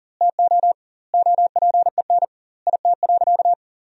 Morse code, heard as a keyed tone, about 40 wpm, 700 Hz. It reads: TO OPEN IT.
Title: 2x Word-Spacing - Sentences from Top 300 Words - 40wpm